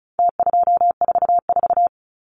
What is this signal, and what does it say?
Morse code, heard as a keyed tone, about 34 wpm, 700 Hz. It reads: T244
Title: Numbers - Contest Style - 4-Digit 35wpm